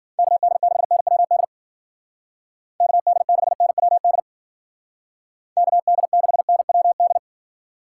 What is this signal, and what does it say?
Morse code, heard as a keyed tone, about 48 wpm, 700 Hz. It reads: KD6NWD KD6NWD KD6NWD
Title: Speed Racing - USA 2x3 Call Signs 40wpm